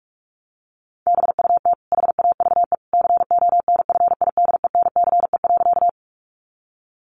Morse code, heard as a keyed tone, about 44 wpm, 700 Hz. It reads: BUT HAVE CONFIDENCE.
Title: Sentences from Top 1500 Words 45wpm